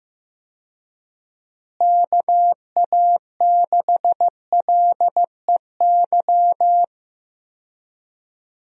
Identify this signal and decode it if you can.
Morse code, heard as a keyed tone, about 15 wpm, 700 Hz. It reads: KA6LEY
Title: USA Call Signs - 6 Characters - Repeated 2x - 15wpm